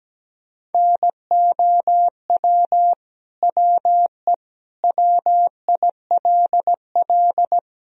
Morse code, heard as a keyed tone, about 17 wpm, 700 Hz. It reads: NOW WE WILL
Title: Sentences from Top 1000 Words - Review Part 1 - 17wpm